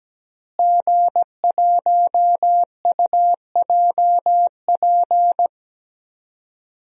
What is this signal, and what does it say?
Morse code, heard as a keyed tone, about 17 wpm, 700 Hz. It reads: G1UJP
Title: DXCC-100 Call Signs - 5 Characters - Repeated 3x - 17wpm